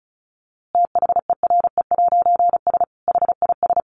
Morse code, heard as a keyed tone, about 34 wpm, 700 Hz. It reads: THERE'S HIS
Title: A Sound Of Thunder 35wpm